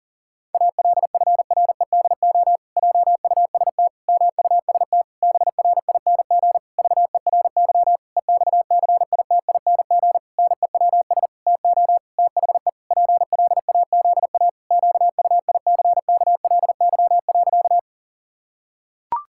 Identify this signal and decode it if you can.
Morse code, heard as a keyed tone, about 40 wpm, 700 Hz. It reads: ALFREDO JUST MUST BRING VERY EXCITING NEWS TO THE PLAZA QUICKLY.
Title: Pangram Challenge v2 - 40wpm